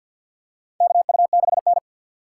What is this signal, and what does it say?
Morse code, heard as a keyed tone, about 51 wpm, 700 Hz. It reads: KU6N
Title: USA Call Signs - 4 Characters - 2x Character Spacing - 50wpm